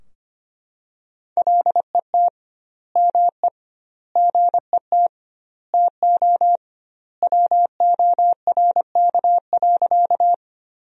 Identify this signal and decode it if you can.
Morse code, heard as a keyed tone, about 25 wpm, 700 Hz. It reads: LET ME GET TO WORK.
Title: 2x Word-Spacing - Sentences from Top 300 Words - 25wpm